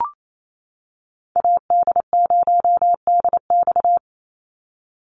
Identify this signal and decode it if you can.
Morse code, heard as a keyed tone, about 28 wpm, 700 Hz. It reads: AD0DX
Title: Call Signs - 28wpm